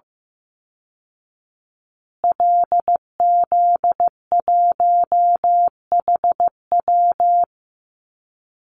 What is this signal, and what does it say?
Morse code, heard as a keyed tone, about 15 wpm, 700 Hz. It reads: LZ1HW
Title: Call Signs Repeated 2x - 15wpm